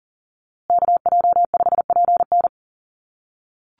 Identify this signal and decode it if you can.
Morse code, heard as a keyed tone, about 40 wpm, 700 Hz. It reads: KJ5PN
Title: USA Call Signs Repeated 2x 40wpm